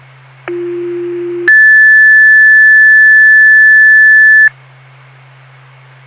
Two-tone page, heard as a one-second tone, then a three-second tone, about 350 then 1750 Hz. It is heard from 0.5 to 4.5 s.
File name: Frankford FD